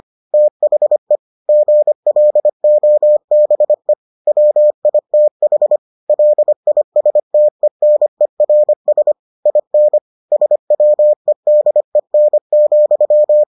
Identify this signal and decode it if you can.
Morse code, heard as a keyed tone, about 25 wpm, 600 Hz. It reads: THE GLOBE WITH LISTENERS IN SWEDEN,